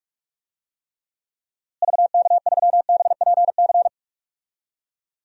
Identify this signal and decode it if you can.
Morse code, heard as a keyed tone, about 46 wpm, 700 Hz. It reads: VK3BPC
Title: DXCC-100 Call Signs - 6 Characters - Repeated 3x - 45wpm